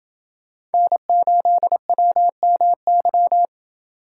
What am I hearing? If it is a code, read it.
Morse code, heard as a keyed tone, about 29 wpm, 700 Hz. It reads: N8WMY